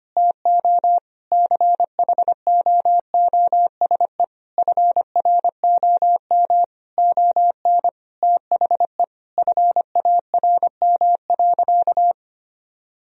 Morse code, heard as a keyed tone, about 25 wpm, 700 Hz. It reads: TO CHOOSE FROM ON THE FARM.
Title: Sentences from Top 1000 Words - Review Part 4 - 25wpm